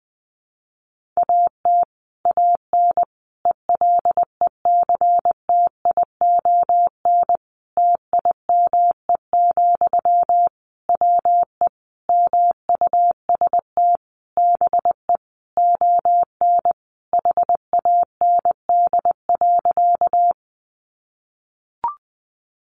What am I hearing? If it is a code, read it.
Morse code, heard as a keyed tone, about 19 wpm, 700 Hz. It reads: AT AN ELECTION TIME, WE MUST BE ON HAND.
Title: Sentences from Top 1500 Words 20wpm